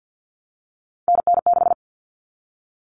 Morse code, heard as a keyed tone, about 50 wpm, 700 Hz. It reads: NN6
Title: Common Cut Numbers - 3-Digit 50wpm